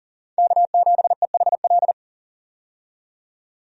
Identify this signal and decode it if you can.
Morse code, heard as a keyed tone, about 39 wpm, 700 Hz. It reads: K7EHL
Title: USA Call Signs - 5 Characters - 40wpm